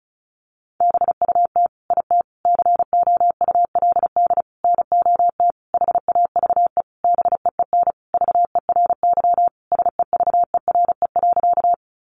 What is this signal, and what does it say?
Morse code, heard as a keyed tone, about 34 wpm, 700 Hz. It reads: BUT IT COULD NOT HAVE BEEN VERY SEVERE.
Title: Sentences from Top 1400 Words 35wpm